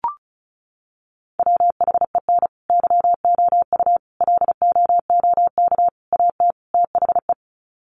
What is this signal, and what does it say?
Morse code, heard as a keyed tone, about 34 wpm, 700 Hz. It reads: WHEN YOU LOOK AT THE
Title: Sets of 5 Words 35wpm